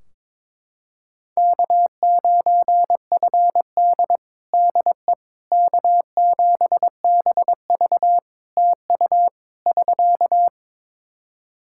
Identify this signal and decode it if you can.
Morse code, heard as a keyed tone, about 22 wpm, 700 Hz. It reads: K9FD DE K7BV TU <SK>
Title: Practice QSO - Exchange 4 - Ending - 22wpm